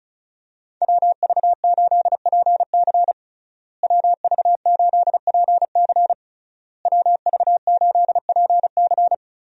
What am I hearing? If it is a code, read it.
Morse code, heard as a keyed tone, about 36 wpm, 700 Hz. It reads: WV8PC WV8PC WV8PC
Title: USA Call Signs - 5 Characters - Repeated 3x - 35wpm